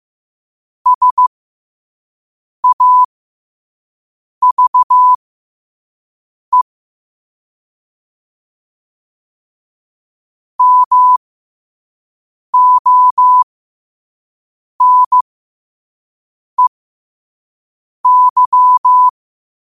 Morse code, heard as a keyed tone, about 15 wpm, spaced as for 5 wpm, 1000 Hz. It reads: SAVE MONEY
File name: qotdfiles / 5wpm